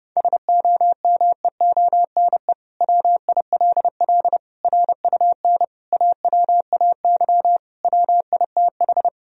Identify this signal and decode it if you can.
Morse code, heard as a keyed tone, about 30 wpm, 700 Hz. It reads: SOMEONE WILL RUN AWAY WITH
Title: Sentences from Top 1000 Words - Review Part 1 - 30wpm